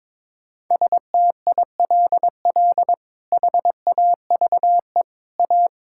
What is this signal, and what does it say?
Morse code, heard as a keyed tone, about 22 wpm, 700 Hz. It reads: STILL HAVE A
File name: practice / 22wpm / Sets of 7 Words - 22wpm